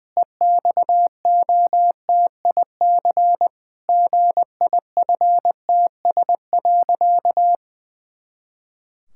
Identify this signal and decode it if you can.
Morse code, heard as a keyed tone, about 20 wpm, 700 Hz. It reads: EXOTIC GIFTS.
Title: Pangram Challenge v2 - 20wpm